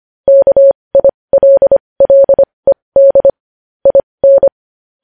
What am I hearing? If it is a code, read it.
Morse code, heard as a keyed tone, about 25 wpm, spaced as for 20 wpm, 550 Hz. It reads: KILLED IN
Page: news-headlines-in-morse-code-at-20-wpm